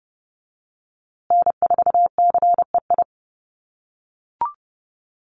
Morse code, heard as a keyed tone, about 30 wpm, 700 Hz. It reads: N4CEI
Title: USA Call Signs Repeated 2x 30wpm